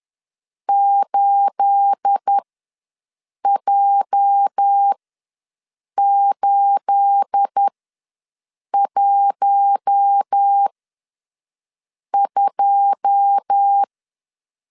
Morse code, heard as a keyed tone, about 10 wpm, spaced as for 6 wpm, 800 Hz. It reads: 8J812
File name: LEZIONI / 9 28 ws 10wpm 4124